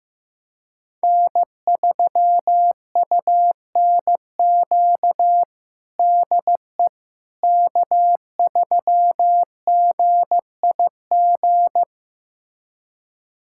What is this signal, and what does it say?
Morse code, heard as a keyed tone, about 15 wpm, 700 Hz. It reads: N3UNQ DE K3GIG
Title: USA Callsign DE Callsign - 5 Characters - 15wpm